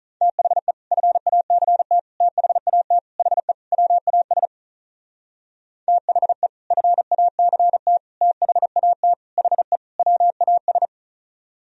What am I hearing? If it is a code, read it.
Morse code, heard as a keyed tone, about 46 wpm, 700 Hz. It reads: THE FACT THAT HE WAS THE FACT THAT HE WAS